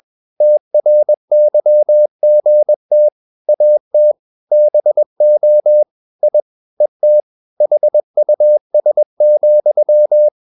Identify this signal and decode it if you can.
Morse code, heard as a keyed tone, about 21 wpm, 600 Hz. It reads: TRYGT AT BO I ET HUS,
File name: Ferie QRS 21wpm